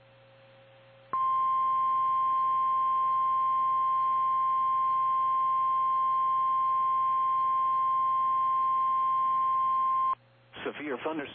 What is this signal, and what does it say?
Emergency Alert System (SAME) tones (the 1050 Hz attention tone), with speech. It is heard from 1.1 to 10.2 s.
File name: eas / KKPRKXPN / KKPRKXPN